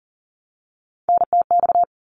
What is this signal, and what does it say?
Morse code, heard as a keyed tone, about 40 wpm, 700 Hz. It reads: NTX
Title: ARRL Sections 40wpm